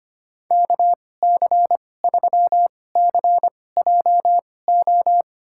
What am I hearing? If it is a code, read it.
Morse code, heard as a keyed tone, about 25 wpm, spaced as for 18 wpm, 700 Hz. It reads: KC3CJO
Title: USA Call Signs - 2x Character Spacing - 25wpm